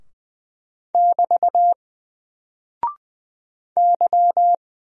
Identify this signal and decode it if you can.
Morse code, heard as a keyed tone, about 20 wpm, 700 Hz. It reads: <BT> Y
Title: Single Character (Letter:Number:Punctuation:Prosign) 20wpm